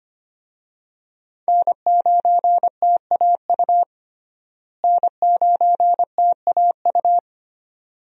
Morse code, heard as a keyed tone, about 25 wpm, 700 Hz. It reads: N9TAU N9TAU
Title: USA Call Signs - 5 Characters - Repeated 2x - 25wpm